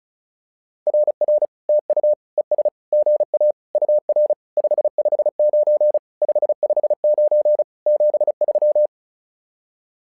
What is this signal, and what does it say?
Morse code, heard as a keyed tone, about 36 wpm, 600 Hz. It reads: RR TU ES GA UR 559 559 73
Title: SOTA QSO Practice 35wpm